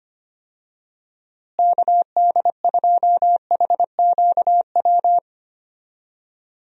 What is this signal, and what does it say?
Morse code, heard as a keyed tone, about 25 wpm, 700 Hz. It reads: KD2HQW
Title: USA Call Signs - 6 Characters - Repeated 2x - 25wpm